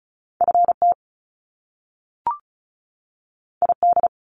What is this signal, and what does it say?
Morse code, heard as a keyed tone, about 36 wpm, 700 Hz. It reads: FT ID
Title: Top Two Letter Combinations in Words 35wpm